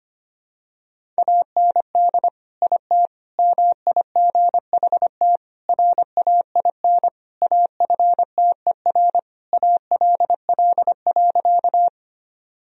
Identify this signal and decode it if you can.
Morse code, heard as a keyed tone, about 25 wpm, 700 Hz. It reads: AND IT MIGHT RAIN AFTER ALL.
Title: Sentences from Top 1000 Words - Review Part 1 - 25wpm